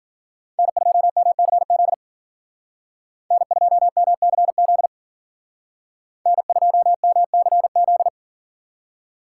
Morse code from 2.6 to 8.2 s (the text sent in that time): N2MCZ N2MCZ